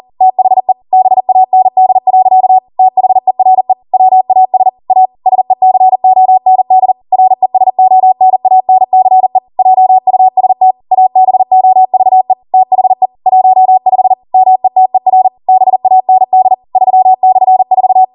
Morse code, heard as a keyed tone, about 40 wpm, 750 Hz. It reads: THE BAND. THERE WAS A SECOND RESONANCE JUST ABOVE THE 15 METER BAND 3/4